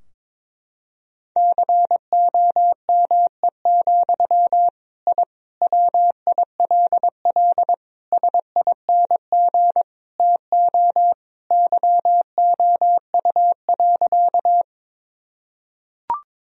Morse code, heard as a keyed tone, about 21 wpm, 700 Hz. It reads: COME, I WILL SING TO YOU.
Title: Sentences from Top 1000 Words - Review Part 2 - 22wpm